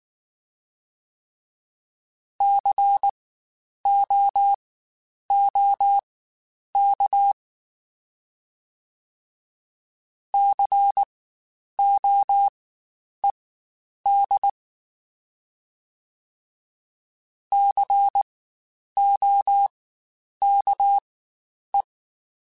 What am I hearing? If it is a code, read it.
Morse code, heard as a keyed tone, about 19 wpm, spaced as for 9 wpm, 800 Hz. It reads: COOK COED COKE
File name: Spaced Alphas / Spaced Alphas mp3s / Cs sp @18